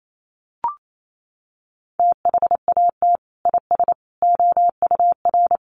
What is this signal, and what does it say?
Morse code, heard as a keyed tone, about 28 wpm, 700 Hz. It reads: THAT IS OUR
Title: Sentences from Top 1500 Words 28wpm